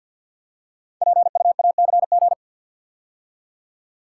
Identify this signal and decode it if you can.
Morse code, heard as a keyed tone, about 51 wpm, 700 Hz. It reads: PUACG
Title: Code Groups - 5 Letters - 50wpm